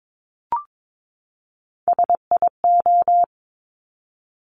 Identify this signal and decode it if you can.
Morse code, heard as a keyed tone, about 22 wpm, 690 Hz. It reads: SIO
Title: Top Three Letter Combinations in Words 22wpm